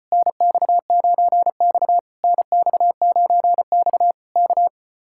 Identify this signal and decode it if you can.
Morse code, heard as a keyed tone, about 33 wpm, 700 Hz. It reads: NX9X NX9X K